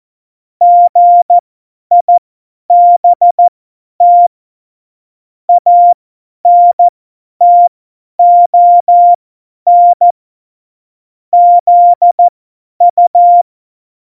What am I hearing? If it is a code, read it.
Morse code, heard as a keyed tone, about 14 wpm, spaced as for 10 wpm, 700 Hz. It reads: GIBT ANTON ZU
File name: Morsekurs / Witze / Witz 12 Mai 2020, 50 F70